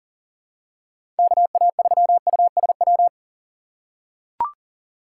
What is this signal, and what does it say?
Morse code, heard as a keyed tone, about 40 wpm, 700 Hz. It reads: KA3USW
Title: USA Call Signs - 6 Characters - 6x Character Spacing - 40wpm